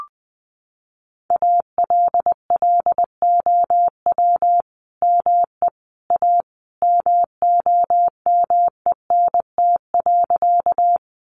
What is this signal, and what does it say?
Morse code, heard as a keyed tone, about 19 wpm, 700 Hz. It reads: ALLOW ME A MOMENT.
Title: Sentences from Top 1000 Words 20wpm